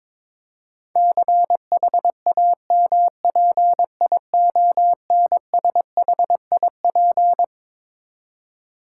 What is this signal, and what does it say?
Morse code, heard as a keyed tone, about 22 wpm, 700 Hz. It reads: CHAMPIONSHIP
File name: Twelve Letter Words - 22wpm